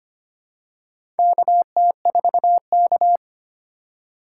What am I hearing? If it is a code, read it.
Morse code, heard as a keyed tone, about 25 wpm, 700 Hz. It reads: KT4K
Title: USA Call Signs - 4x Character Spacing - 25wpm